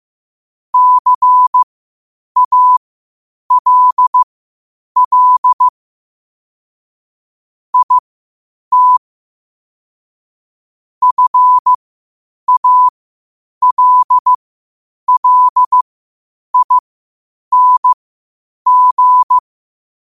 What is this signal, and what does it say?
Morse code, heard as a keyed tone, about 15 wpm, spaced as for 8 wpm, 1000 Hz. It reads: CALL IT FALLING
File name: qotdfiles / 8wpm